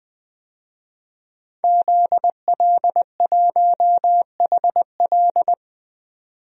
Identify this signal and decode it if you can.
Morse code, heard as a keyed tone, about 21 wpm, 700 Hz. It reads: ZL1HL
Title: DXCC-100 Call Signs - 5 Characters - 20wpm